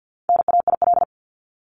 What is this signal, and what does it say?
Morse code, heard as a keyed tone, about 50 wpm, 700 Hz. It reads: NAIL
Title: Top 1501-1750 Words 50wpm